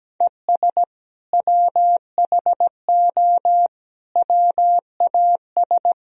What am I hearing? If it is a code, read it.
Morse code, heard as a keyed tone, about 17 wpm, 700 Hz. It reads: ES WHO WAS
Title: Sentences from Top 100 Words and Abbreviations 17wpm